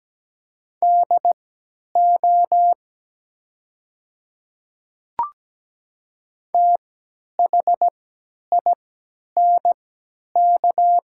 Morse code, heard as a keyed tone, about 17 wpm, spaced as for 10 wpm, 700 Hz. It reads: DO THINK
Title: 3x Character Spacing - Top 200 Words - 17wpm